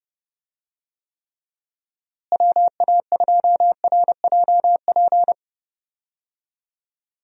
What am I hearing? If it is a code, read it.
Morse code, heard as a keyed tone, about 32 wpm, 700 Hz. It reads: WA2RJP
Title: USA Call Signs - 6 Characters - Repeated 2x - 30wpm